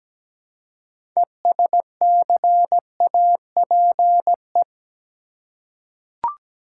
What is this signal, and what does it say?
Morse code, heard as a keyed tone, about 17 wpm, 700 Hz. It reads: ESCAPE
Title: Six Letter Words - 17wpm